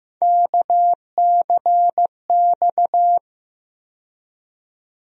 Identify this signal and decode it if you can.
Morse code, heard as a keyed tone, about 16 wpm, 700 Hz. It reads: KCX
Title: USA Call Sign Postfixes - 3 Letters 15wpm